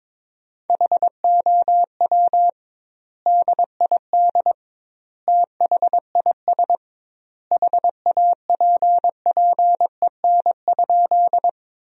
Morse code, heard as a keyed tone, about 22 wpm, 700 Hz. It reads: HOW DID THIS HAPPEN?
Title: 2x Word-Spacing - Sentences from Top 400 Words - 22wpm